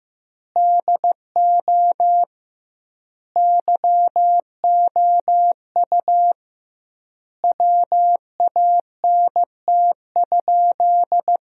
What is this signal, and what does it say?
Morse code, heard as a keyed tone, about 15 wpm, 700 Hz. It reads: DO YOU WANT?
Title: 2x Word-Spacing - Sentences from Top 200 Words - 15wpm